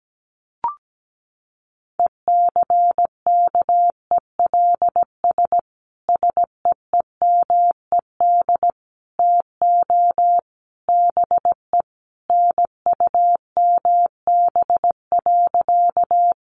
Morse code, heard as a keyed tone, about 17 wpm, 700 Hz. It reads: ECKELS SEEMED TO BE NUMB.
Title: A Sound Of Thunder 17wpm